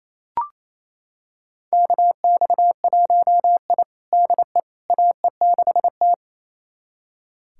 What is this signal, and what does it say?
Morse code, heard as a keyed tone, about 28 wpm, 700 Hz. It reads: KX1I DE AE6T
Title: USA Callsign DE Callsign - 4 Characters - 28wpm